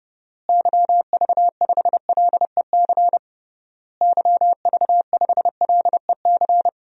Morse code, heard as a keyed tone, about 30 wpm, 700 Hz. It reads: YV5LEC YV5LEC
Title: DXCC-100 Call Signs - 6 Characters - Repeated 3x - 30wpm